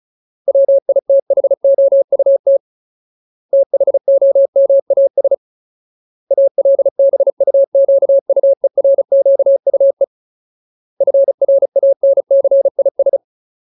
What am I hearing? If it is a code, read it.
Morse code, heard as a keyed tone, about 34 wpm, 550 Hz. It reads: WITHOUT THOMAS ALBUQUERQUE FRANCIS
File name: Ultimate Word List / 35WPM Ultimate